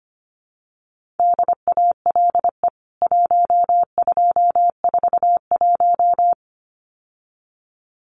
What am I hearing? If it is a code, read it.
Morse code, heard as a keyed tone, about 25 wpm, 700 Hz. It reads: DALE 1241
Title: CWT Exchange 25wpm